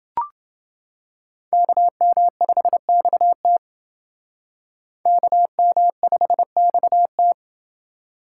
Morse code, heard as a keyed tone, about 28 wpm, 700 Hz. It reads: KM5XT KM5XT